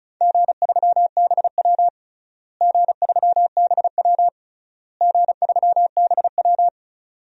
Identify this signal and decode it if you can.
Morse code, heard as a keyed tone, about 34 wpm, 700 Hz. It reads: G3BW G3BW G3BW
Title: DXCC-100 Call Signs - 4 Characters - Repeated 3x - 35wpm